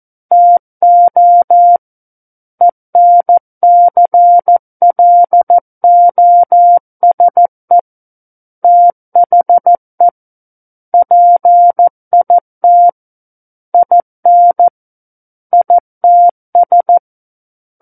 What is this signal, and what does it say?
Morse code, heard as a keyed tone, about 14 wpm, 700 Hz. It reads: TO ENCLOSE THE PIT IN ITS
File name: Book One / 05-Chapter 5 - 14 WPM